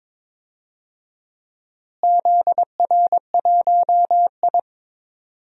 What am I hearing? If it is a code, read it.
Morse code, heard as a keyed tone, about 22 wpm, 700 Hz. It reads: ZR1I
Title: DXCC-100 Call Signs - 4 Characters - Repeated 3x - 22wpm